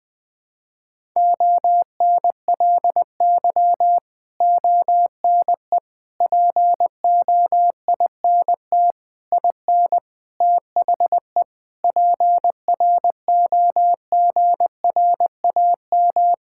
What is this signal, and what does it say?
Morse code, heard as a keyed tone, about 21 wpm, 700 Hz. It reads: ONLY ONE POINT IN THE PROGRAM
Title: Sentences from Top 1000 Words - Review Part 5 - 20wpm